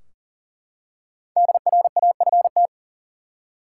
Morse code, heard as a keyed tone, about 40 wpm, 700 Hz. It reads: DRAFT
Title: Five Letter Words - 40wpm